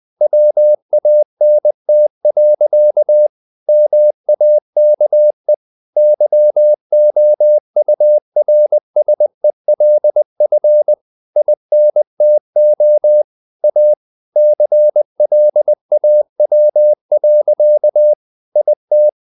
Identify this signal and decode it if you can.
Morse code, heard as a keyed tone, about 20 wpm, 600 Hz. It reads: WANT. MAKE YOURSELF INTO A CLAW. IT
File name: the old man and the sea / the old man and the sea part 1 20wpm